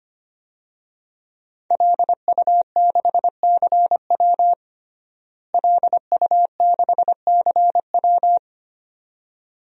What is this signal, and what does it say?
Morse code, heard as a keyed tone, about 25 wpm, 700 Hz. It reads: LU6CW LU6CW
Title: DXCC-100 Call Signs - 5 Characters - Repeated 2x - 25wpm